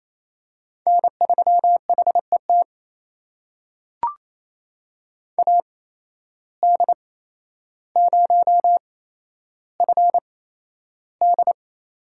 Morse code, heard as a keyed tone, about 28 wpm, spaced as for 12 wpm, 700 Hz. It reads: N3HET AD0FD
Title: USA Call Signs - 8x Character Spacing - 28wpm